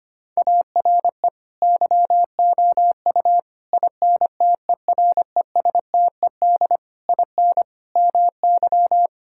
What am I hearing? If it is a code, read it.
Morse code, heard as a keyed tone, about 25 wpm, 700 Hz. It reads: ARE YOU INTERESTED IN MY
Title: Sentences from Top 1000 Words - Review Part 6 - 25wpm